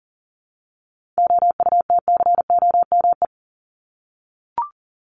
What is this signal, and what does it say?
Morse code, heard as a keyed tone, about 40 wpm, 700 Hz. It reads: OUTCOME
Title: Top 1501-1750 Words 40wpm